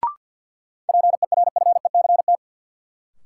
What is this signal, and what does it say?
Morse code, heard as a keyed tone, about 52 wpm, 700 Hz. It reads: PERFECT
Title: Sentences from Top 1000 Words - New Word Drill - 50wpm